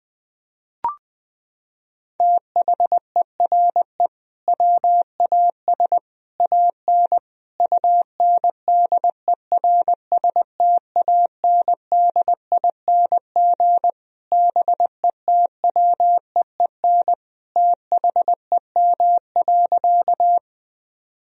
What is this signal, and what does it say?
Morse code, heard as a keyed tone, about 21 wpm, 700 Hz. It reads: THERE WAS AN UNDERSTANDING BETWEEN THEM.
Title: Sentences from Top 1000 Words - Review Part 3 - 20wpm